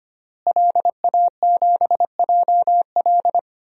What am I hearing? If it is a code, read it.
Morse code, heard as a keyed tone, about 25 wpm, 700 Hz. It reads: LA7JL
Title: DXCC-100 Call Signs - 5 Characters - Repeated 3x - 25wpm